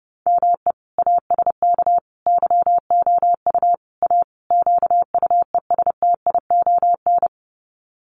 Morse code, heard as a keyed tone, about 30 wpm, 700 Hz. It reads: ME ASK YOU A QUESTION
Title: Sets of 5 Words Encore2 30wpm